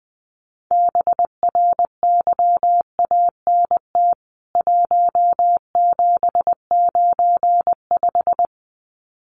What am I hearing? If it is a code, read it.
Morse code, heard as a keyed tone, about 20 wpm, 700 Hz. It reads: BRYANT 1795
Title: CWT Exchange 20wpm